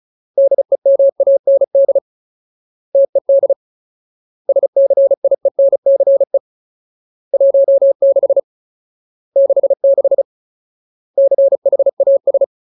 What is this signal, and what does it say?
Morse code, heard as a keyed tone, about 34 wpm, 550 Hz. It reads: DEMAND TED SCIENCE 16 66 CHAS
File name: Ultimate Word List / 35WPM Ultimate